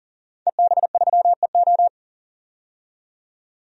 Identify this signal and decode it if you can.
Morse code, heard as a keyed tone, about 40 wpm, 700 Hz. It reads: EB3EO
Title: DXCC-100 Call Signs - 5 Characters - 40wpm